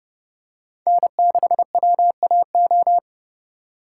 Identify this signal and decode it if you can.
Morse code, heard as a keyed tone, about 29 wpm, 700 Hz. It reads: N6WAO